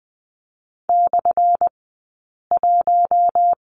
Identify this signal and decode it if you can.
Morse code, heard as a keyed tone, about 20 wpm, spaced as for 8 wpm, 700 Hz. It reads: /1
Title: 211 Lesson 34d - Number 8 - Two Characters 20wpm